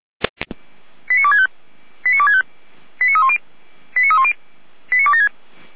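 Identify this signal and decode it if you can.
Five-tone selective call (ZVEI), heard as quick runs of five tones, tones of about 70 ms. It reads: 89265 89265 89310 89310 89276
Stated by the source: ZVEI five-tone selective call 89067 received by SDR, from the feuerwehr-orlamuende fire brigade audio archive.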